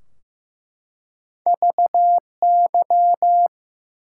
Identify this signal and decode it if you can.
Morse code, heard as a keyed tone, about 15 wpm, 700 Hz. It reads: VY
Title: Top 90 QSO Elements 15wpm